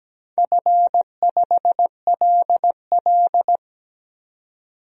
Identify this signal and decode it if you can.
Morse code, heard as a keyed tone, about 17 wpm, 700 Hz. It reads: F5LL
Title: DXCC-100 Call Signs - 4 Characters - 17wpm